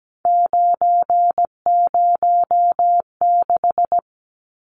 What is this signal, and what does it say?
Morse code, heard as a keyed tone, about 17 wpm, 700 Hz. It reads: 906
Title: Numbers - 3-Digit 17wpm